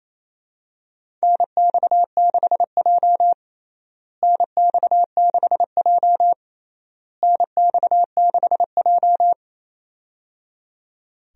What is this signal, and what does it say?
Morse code, heard as a keyed tone, about 27 wpm, 700 Hz. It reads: NX6J NX6J NX6J
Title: USA Call Signs - 4 Characters - Repeated 3x - 28wpm